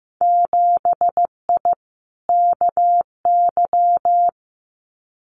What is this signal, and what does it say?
Morse code, heard as a keyed tone, about 15 wpm, 700 Hz. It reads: 7I KY
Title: Winter Field Day Exchanges Encore 15wpm